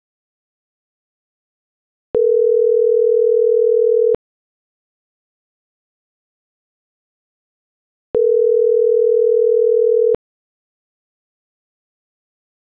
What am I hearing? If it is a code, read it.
Ringback tone (North America), 440 + 480 Hz.